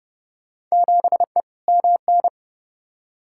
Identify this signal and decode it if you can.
Morse code, heard as a keyed tone, about 32 wpm, 700 Hz. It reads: 7E MN
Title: Field Day Exchanges Encore - 30wpm